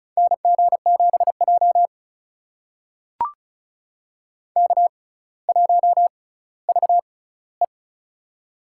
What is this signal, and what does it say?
Morse code, heard as a keyed tone, about 36 wpm, spaced as for 18 wpm, 700 Hz. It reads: NG7J K1VE
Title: USA Call Signs - 4 Characters - 6x Character Spacing - 35wpm